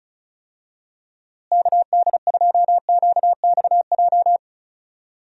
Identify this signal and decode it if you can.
Morse code, heard as a keyed tone, about 34 wpm, 700 Hz. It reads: KD2QXJ
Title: USA Call Signs - 6 Characters - Repeated 2x - 35wpm